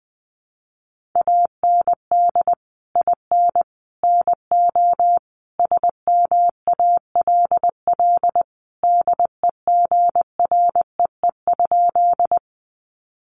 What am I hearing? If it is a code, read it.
Morse code, heard as a keyed tone, about 19 wpm, 700 Hz. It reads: AND IN NO SMALL DEGREE?
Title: Sentences from Top 1500 Words 20wpm